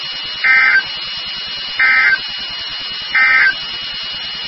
Emergency Alert System (SAME) tones, heard as harsh buzzing bursts. It is heard from 0.3 to 3.6 s.
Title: NOAA Weather Radio WXK36